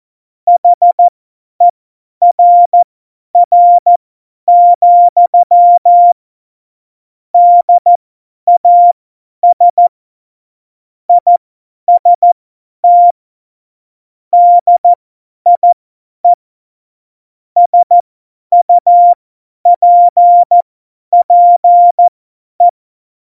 Morse code, heard as a keyed tone, about 14 wpm, spaced as for 10 wpm, 700 Hz. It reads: HERR, DAS IST DIE SUPPE
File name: Morsekurs / Witze / Witz 10.Dezember 2019, 50 F70